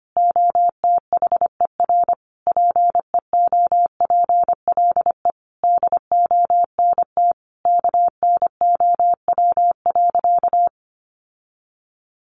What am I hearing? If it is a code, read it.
Morse code, heard as a keyed tone, about 25 wpm, 700 Hz. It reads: OTHER PEOPLE DONT KNOW.
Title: Sentences from Top 600 Words 25wpm